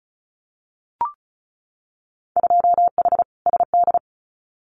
Morse code, heard as a keyed tone, about 34 wpm, 700 Hz. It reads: 2H SD
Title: Winter Field Day Exchanges 35wpm